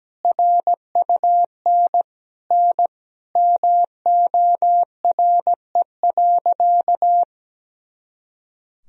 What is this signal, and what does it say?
Morse code, heard as a keyed tone, about 17 wpm, 700 Hz. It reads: RUN N MORE.
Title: Sentences from Top 300 Words and Abbreviations 17wpm